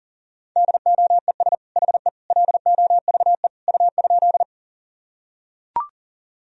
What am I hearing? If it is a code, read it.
Morse code, heard as a keyed tone, about 40 wpm, 700 Hz. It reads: DOES HE LOVE U?
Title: Sentences from Top 400 Words and Abbreviations 40wpm